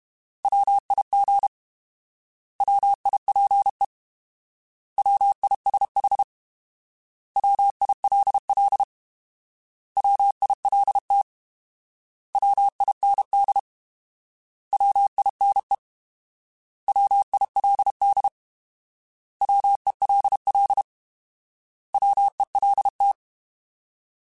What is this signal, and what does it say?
Morse code, heard as a keyed tone, about 32 wpm, 800 Hz. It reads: WIG WIPE WISH WILL WILT WIND WINE WILD WELL WELT
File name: Alphabetical / Alphabetical mp3s / Ws @30